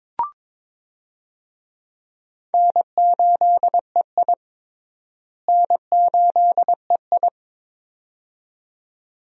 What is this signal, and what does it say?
Morse code, heard as a keyed tone, about 21 wpm, 700 Hz. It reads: N8EI N8EI
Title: USA Call Signs - 4 Characters - Repeated 2x - 22wpm